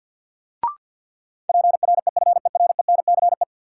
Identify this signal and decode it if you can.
Morse code, heard as a keyed tone, about 53 wpm, 700 Hz. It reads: PREFERENCE
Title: Sentences from Top 1500 Words - New Word Drill - 50wpm